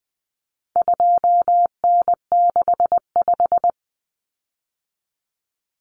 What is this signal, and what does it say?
Morse code, heard as a keyed tone, about 20 wpm, 700 Hz. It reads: 2N65
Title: Common Cut Numbers - 4-Digit 20wpm